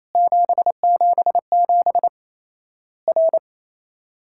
Morse code, heard as a keyed tone, about 28 wpm, 700 Hz. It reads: 777 R
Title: Worked All Europe DX Contest - QTC Exchange - 28wpm